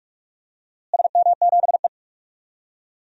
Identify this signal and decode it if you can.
Morse code, heard as a keyed tone, about 44 wpm, 700 Hz. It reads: SM7E
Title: DXCC-100 Call Signs - 4 Characters - Repeated 3x - 45wpm